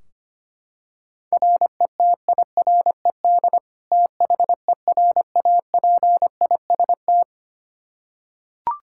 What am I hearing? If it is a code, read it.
Morse code, heard as a keyed tone, about 25 wpm, 700 Hz. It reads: RETIRED THERAPIST
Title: Practice QSO - Exchange 3 - More Info - 25wpm